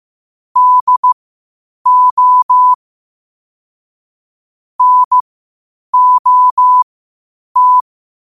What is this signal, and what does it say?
Morse code, heard as a keyed tone, about 15 wpm, spaced as for 8 wpm, 1000 Hz. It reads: DO NOT
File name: qotdfiles / 8wpm